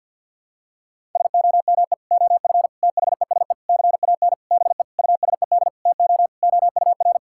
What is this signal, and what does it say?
Morse code, heard as a keyed tone, about 52 wpm, 700 Hz. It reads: SOME OF THESE CAN BE USED TO OUR
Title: Sentences from Top 1000 Words - Review Part 6 - 50wpm